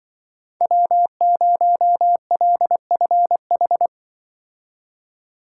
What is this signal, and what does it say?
Morse code, heard as a keyed tone, about 23 wpm, 700 Hz. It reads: W0LFH
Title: Speed Racing - USA Call Signs 20wpm